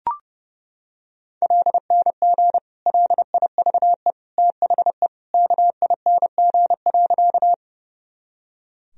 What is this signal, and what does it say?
Morse code, heard as a keyed tone, about 30 wpm, 700 Hz. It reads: LNG LIVE THE KING.
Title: Sentences from Top 400 Words and Abbreviations 30wpm